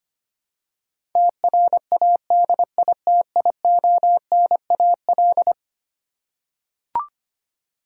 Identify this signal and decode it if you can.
Morse code, heard as a keyed tone, about 25 wpm, 700 Hz. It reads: TRADITIONAL
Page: morse-code-ninja-practice-25wpm